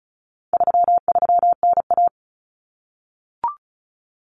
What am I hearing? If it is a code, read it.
Morse code, heard as a keyed tone, about 34 wpm, 700 Hz. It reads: 33NA
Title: Common Cut Numbers - 4-Digit 35wpm